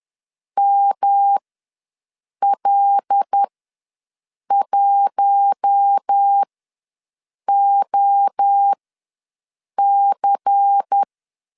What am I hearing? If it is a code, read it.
Morse code, heard as a keyed tone, about 10 wpm, spaced as for 6 wpm, 800 Hz. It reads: ML1OC
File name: LEZIONI / 5 LBVC / 5 LBVC ws 10wpm 3733